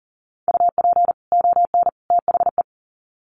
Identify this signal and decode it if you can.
Morse code, heard as a keyed tone, about 40 wpm, 700 Hz. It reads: UP ON THE
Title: Sets of 3 Words 40wpm